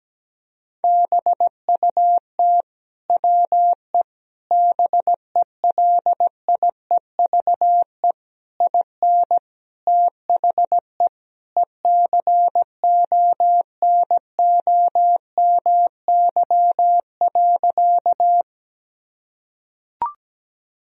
Morse code, heard as a keyed tone, about 17 wpm, 700 Hz. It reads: BUT WE BELIEVE IN THE ECONOMY.
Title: Sentences from Top 1000 Words - Review Part 3 - 17wpm